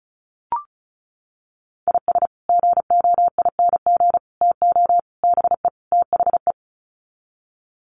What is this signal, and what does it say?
Morse code, heard as a keyed tone, about 34 wpm, 700 Hz. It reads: IS GOING TO BE THE
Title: Sets of 5 Words 35wpm